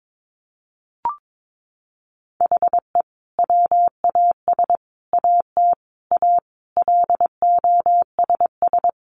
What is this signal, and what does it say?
Morse code, heard as a keyed tone, about 22 wpm, 700 Hz. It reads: HE WAS AT A LOSS
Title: Sentences from Top 1400 Words 22wpm